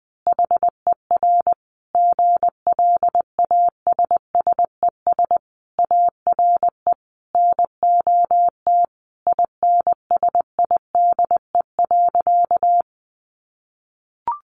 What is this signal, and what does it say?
Morse code, heard as a keyed tone, about 20 wpm, 700 Hz. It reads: HER GLASSES ARE NOT INSIDE.
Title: Additional Easy Phrases 20wpm